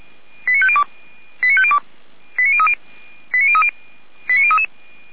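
Five-tone selective call (ZVEI), heard as quick runs of five tones, tones of about 80 ms. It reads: 89502 89502 89030 89030 89033
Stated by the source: ZVEI five-tone selective call received on an SDR.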